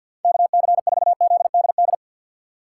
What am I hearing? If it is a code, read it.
Morse code, heard as a keyed tone, about 49 wpm, 700 Hz. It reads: KC4ZDD
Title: USA Call Signs - 6 Characters - 4x Character Spacing - 50wpm